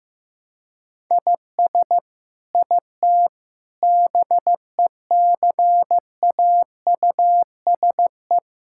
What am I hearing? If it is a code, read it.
Morse code, heard as a keyed tone, about 15 wpm, 700 Hz. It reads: IS IT BECAUSE
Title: Sentences from Top 1000 Words - Review Part 5 - 15wpm